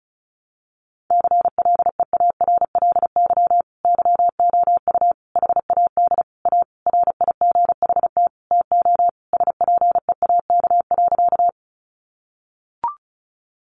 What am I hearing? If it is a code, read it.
Morse code, heard as a keyed tone, about 34 wpm, 700 Hz. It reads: CLEARLY YOU HAD A RIGHT TO SPEAK.
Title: Sentences from Top 1300 Words 35wpm